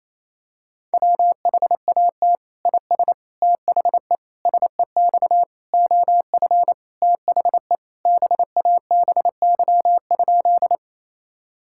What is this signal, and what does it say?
Morse code, heard as a keyed tone, about 28 wpm, 700 Hz. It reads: WHAT IS THE SEX OF THE BABY?
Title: Sentences from Top 1600 Words 28wpm